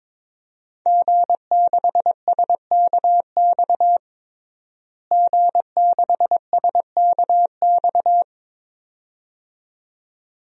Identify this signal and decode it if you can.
Morse code, heard as a keyed tone, about 22 wpm, 700 Hz. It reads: G6SKX G6SKX
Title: DXCC-100 Call Signs - 5 Characters - Repeated 2x - 22wpm